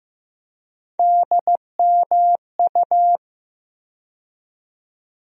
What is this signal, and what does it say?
Morse code, heard as a keyed tone, about 15 wpm, 700 Hz. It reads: DMU
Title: USA Call Sign Postfixes - 3 Letters 15wpm